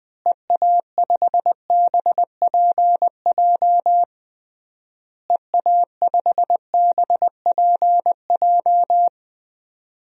Morse code, heard as a keyed tone, about 20 wpm, 700 Hz. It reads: EA5BPJ EA5BPJ
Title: DXCC-100 Call Signs - 6 Characters - Repeated 3x - 20wpm